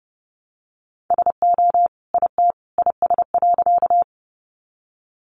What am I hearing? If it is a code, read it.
Morse code, heard as a keyed tone, about 30 wpm, 700 Hz. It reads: SO IT IS.
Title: Sentences from Top 100 Words 30wpm